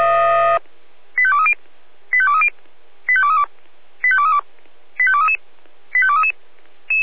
Five-tone selective call (ZVEI), heard as quick runs of five tones, tones of about 70 ms. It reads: 85230 85230 85232 85232 85233 85233